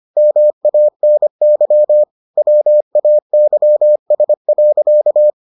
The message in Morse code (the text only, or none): MANY WAYS.